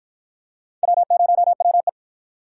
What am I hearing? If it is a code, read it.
Morse code, heard as a keyed tone, about 53 wpm, 700 Hz. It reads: W0WE